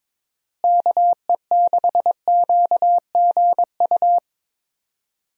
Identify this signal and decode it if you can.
Morse code, heard as a keyed tone, about 22 wpm, 700 Hz. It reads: KE6QGU
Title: USA Call Signs - 6 Characters - Repeated 3x - 22wpm